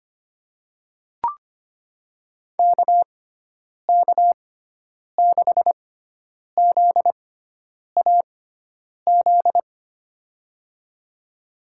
Morse code, heard as a keyed tone, about 25 wpm, spaced as for 9 wpm, 700 Hz. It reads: KK6ZAZ
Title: USA Call Signs - 6x Character Spacing - 25wpm